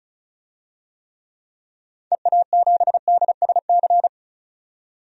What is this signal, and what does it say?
Morse code, heard as a keyed tone, about 34 wpm, 700 Hz. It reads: EA7DSC
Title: DXCC-100 Call Signs - 6 Characters - 35wpm